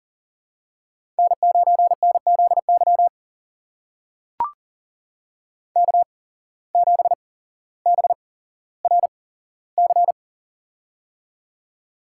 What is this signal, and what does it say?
Morse code, heard as a keyed tone, about 42 wpm, spaced as for 17 wpm, 700 Hz. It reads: N9NZY K7BRC
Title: USA Call Signs - 8x Character Spacing - 40wpm